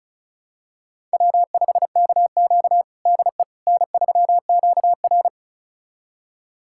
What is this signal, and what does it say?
Morse code, heard as a keyed tone, about 36 wpm, 700 Hz. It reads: W5KQ DE N3QR
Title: USA Callsign DE Callsign - 4 Characters - 35wpm